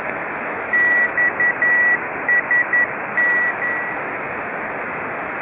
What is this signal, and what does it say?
Morse code, heard as a keyed tone, about 11 wpm, 1950 Hz, with noise. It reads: XSN